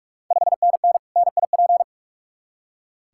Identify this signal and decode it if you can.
Morse code, heard as a keyed tone, about 46 wpm, 700 Hz. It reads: 5NN NIP
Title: Ontario QSO Party Exchange 45wpm